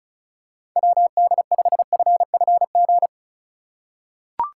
Morse code, heard as a keyed tone, about 36 wpm, 700 Hz. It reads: WD5FFG
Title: USA Call Signs - 6 Characters - 4x Character Spacing - 35wpm